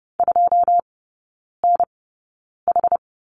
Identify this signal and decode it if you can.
Morse code, heard as a keyed tone, about 30 wpm, spaced as for 9 wpm, 700 Hz. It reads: 2NH
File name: 081 Lesson 14e - Number 2 - Three Characters 30wpm